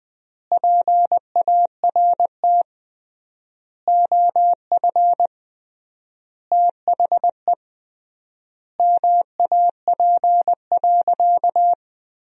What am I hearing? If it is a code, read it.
Morse code, heard as a keyed tone, about 21 wpm, 700 Hz. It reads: PART OF THE MAP.
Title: 3x Word-Spacing - Sentences from Top 400 Words - 20wpm